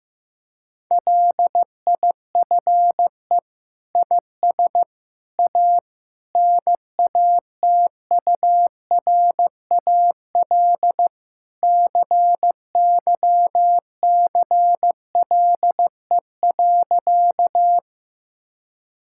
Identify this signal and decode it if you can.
Morse code, heard as a keyed tone, about 15 wpm, 700 Hz. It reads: LIFE IS A NATURAL CYCLE.
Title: Sentences from Top 1000 Words - Review Part 6 - 15wpm